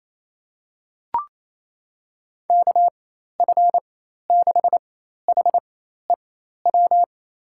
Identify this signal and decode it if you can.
Morse code, heard as a keyed tone, about 27 wpm, spaced as for 13 wpm, 700 Hz. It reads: KF6HEW
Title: USA Call Signs - 4x Character Spacing - 28wpm